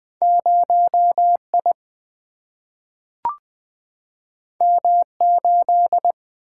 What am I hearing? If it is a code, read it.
Morse code, heard as a keyed tone, about 20 wpm, 700 Hz. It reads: 0 I M 8
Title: Single-Letter Single-Number 20wpm